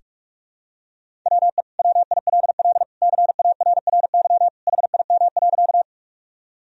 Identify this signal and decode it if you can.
Morse code, heard as a keyed tone, about 46 wpm, 700 Hz. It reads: WE WILL CARRY HIM.
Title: Sentences from Top 1000 Words - Review Part 1 - 45wpm